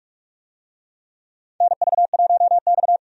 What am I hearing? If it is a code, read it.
Morse code, heard as a keyed tone, about 46 wpm, 700 Hz. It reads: NV1X
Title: USA Call Signs - 4 Characters - 45wpm